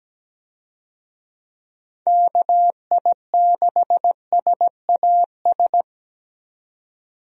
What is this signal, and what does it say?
Morse code, heard as a keyed tone, about 17 wpm, 700 Hz. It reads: KI6SAS
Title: USA Call Signs - 6 Characters - 17wpm